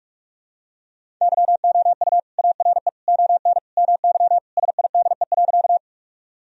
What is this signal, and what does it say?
Morse code, heard as a keyed tone, about 42 wpm, 700 Hz. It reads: YOU ARE ON MY SIDE.
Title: Sentences from Top 1000 Words - Review Part 1 - 45wpm